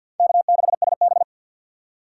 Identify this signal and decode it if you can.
Morse code, heard as a keyed tone, about 52 wpm, 700 Hz. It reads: K6SB
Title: USA Call Signs - 4 Characters - 6x Character Spacing - 50wpm